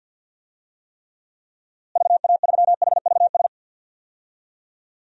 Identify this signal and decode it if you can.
Morse code, heard as a keyed tone, about 45 wpm, 700 Hz. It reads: VA3HVS
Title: DXCC-100 Call Signs - 6 Characters - Repeated 2x - 50wpm